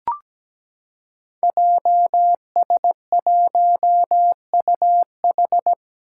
Morse code, heard as a keyed tone, about 17 wpm, 700 Hz. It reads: JS1UH
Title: Five Challenging Letter-Number Combinations 17wpm